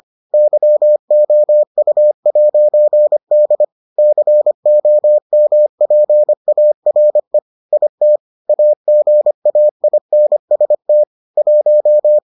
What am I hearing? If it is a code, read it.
Morse code, heard as a keyed tone, about 25 wpm, 600 Hz. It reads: YOU'D COMPARE IT AGAINST 1